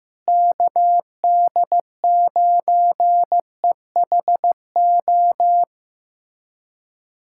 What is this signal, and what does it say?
Morse code, heard as a keyed tone, about 15 wpm, 700 Hz. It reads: KD9EHO
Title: USA Callsign DE Callsign 15wpm